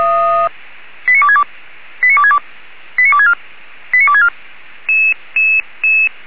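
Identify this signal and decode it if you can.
Five-tone selective call (ZVEI), heard as quick runs of five tones, tones of about 70 ms. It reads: 89262 89262 89264 89264
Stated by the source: ZVEI five-tone call received on an SDR.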